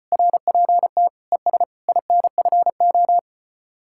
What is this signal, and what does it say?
Morse code, heard as a keyed tone, about 32 wpm, 700 Hz. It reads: RPT ES INFO